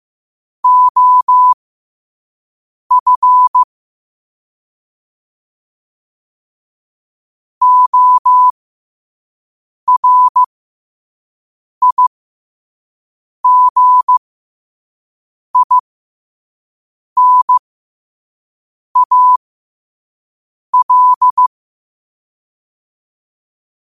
Morse code, heard as a keyed tone, about 15 wpm, spaced as for 5 wpm, 1000 Hz. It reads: OF ORIGINAL
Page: qotd-at-5-wpm